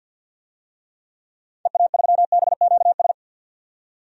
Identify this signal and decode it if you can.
Morse code, heard as a keyed tone, about 51 wpm, 700 Hz. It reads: EA3BQS
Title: DXCC-100 Call Signs - 6 Characters - 50wpm